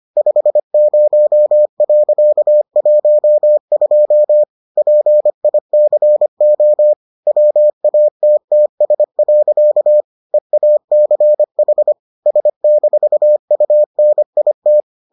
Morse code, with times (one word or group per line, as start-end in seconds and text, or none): 0.2-4.4 50.12
4.8-6.9 PICO
7.3-10.0 WATTS.
10.3-11.9 EACH
12.3-14.8 S-UNIT